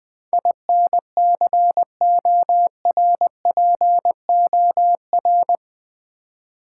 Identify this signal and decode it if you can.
Morse code, heard as a keyed tone, about 20 wpm, 700 Hz. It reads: INCORPOR
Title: Build A Word - 4 Letters At A Time - 11-Letter Words - 20wpm